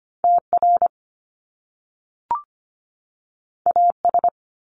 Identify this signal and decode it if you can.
Morse code, heard as a keyed tone, about 25 wpm, 700 Hz. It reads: T R A S
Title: Top Two Letter Combinations in Words 25wpm